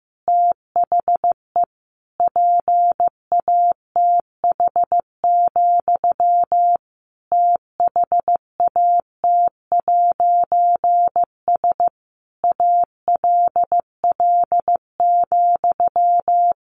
Morse code, heard as a keyed tone, about 15 wpm, 700 Hz. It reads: THE PATH, THAT'S ALL,
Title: A Sound Of Thunder 15wpm